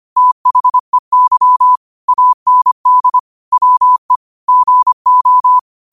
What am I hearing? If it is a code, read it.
Morse code, heard as a keyed tone, about 25 wpm, 1000 Hz. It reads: THEY AND WE GO